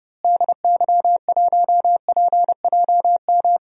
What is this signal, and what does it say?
Morse code, heard as a keyed tone, about 31 wpm, 700 Hz. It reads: DY1PJM
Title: DXCC-100 Call Signs - 6 Characters - Repeated 3x - 30wpm